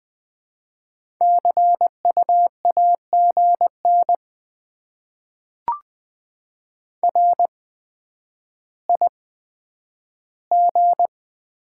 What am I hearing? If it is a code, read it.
Morse code, heard as a keyed tone, about 20 wpm, spaced as for 10 wpm, 700 Hz. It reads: CU AGN RIG
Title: Top 20 QSO Elements - 8x Character Spacing - 20wpm